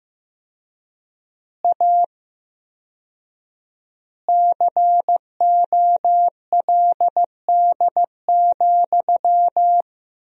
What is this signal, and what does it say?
Morse code, heard as a keyed tone, about 15 wpm, 700 Hz. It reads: A COLD,
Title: 4x Word-Spacing - Sentences from Top 500 Words - 15wpm